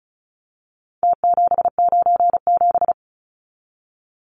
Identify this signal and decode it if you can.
Morse code, heard as a keyed tone, about 34 wpm, 700 Hz. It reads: T797
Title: Numbers - Contest Style - 4-Digit 35wpm